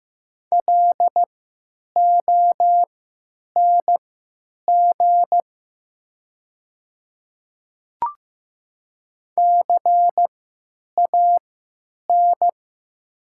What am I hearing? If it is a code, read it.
Morse code, heard as a keyed tone, about 15 wpm, spaced as for 9 wpm, 700 Hz. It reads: LONG CAN
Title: 3x Character Spacing - Top 100 Words - 15wpm